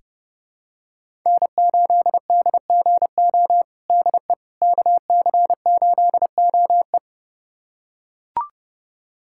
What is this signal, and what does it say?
Morse code, heard as a keyed tone, about 32 wpm, 700 Hz. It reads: N8DGO DE KC8OE
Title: USA Callsign DE Callsign - 5 Characters - 30wpm